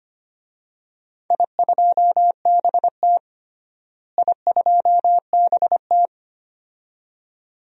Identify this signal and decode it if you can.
Morse code, heard as a keyed tone, about 25 wpm, 700 Hz. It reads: I2BT I2BT
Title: DXCC-100 Call Signs - 4 Characters - Repeated 2x - 25wpm